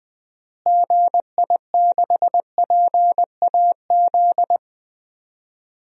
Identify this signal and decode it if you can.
Morse code, heard as a keyed tone, about 20 wpm, 700 Hz. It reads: GI6PAZ
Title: DXCC-100 Call Signs - 6 Characters - Repeated 3x - 20wpm